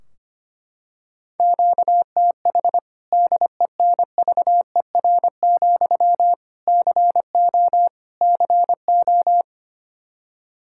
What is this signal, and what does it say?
Morse code, heard as a keyed tone, about 25 wpm, 700 Hz. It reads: QTH DENVER, CO CO
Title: Practice QSO - Exchange 1 - Beginning - 25wpm